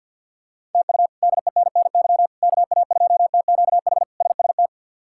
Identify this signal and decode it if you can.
Morse code, heard as a keyed tone, about 52 wpm, 690 Hz. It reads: TU DENNY KA2TQH SST
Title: K1USN SST QSO Practice - 50wpm